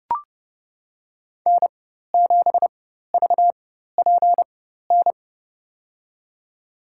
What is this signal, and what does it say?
Morse code, heard as a keyed tone, about 30 wpm, spaced as for 14 wpm, 700 Hz. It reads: N7VPN
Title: USA Call Signs - 5 Characters - 4x Character Spacing - 30wpm